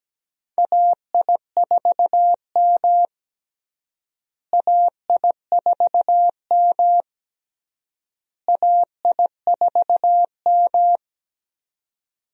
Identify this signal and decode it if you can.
Morse code, heard as a keyed tone, about 17 wpm, 700 Hz. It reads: AI4M AI4M AI4M
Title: USA Call Signs - 4 Characters - Repeated 3x - 17wpm